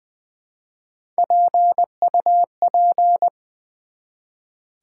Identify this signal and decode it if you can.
Morse code, heard as a keyed tone, about 20 wpm, 700 Hz. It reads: PUP
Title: Build A Word - 1 Letter At A Time - 5-Letter Words - 20wpm